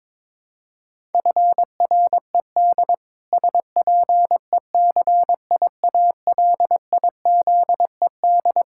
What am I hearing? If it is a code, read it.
Morse code, heard as a keyed tone, about 21 wpm, 700 Hz. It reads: FRED SPECIALIZED
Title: Pangram Challenge v2 - 22wpm